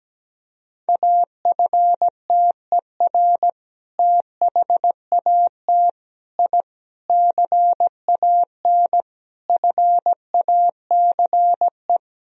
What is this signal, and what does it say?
Morse code, heard as a keyed tone, about 17 wpm, 700 Hz. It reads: AFTER THAT I CAN FACE
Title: Sentences from Top 1000 Words - Review Part 2 - 17wpm